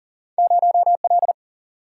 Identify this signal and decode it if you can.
Morse code, heard as a keyed tone, about 42 wpm, 700 Hz. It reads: 0L
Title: Single-Letter Single-Number 40wpm